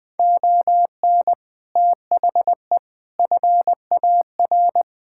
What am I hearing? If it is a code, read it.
Morse code, heard as a keyed tone, about 21 wpm, 700 Hz. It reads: ON THE FAR
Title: Sets of 6 Words - 20wpm